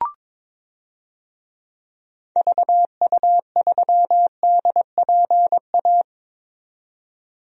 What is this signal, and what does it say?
Morse code, heard as a keyed tone, about 22 wpm, 700 Hz. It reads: VU3DPA
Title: DXCC-100 Call Signs - 6 Characters - 22wpm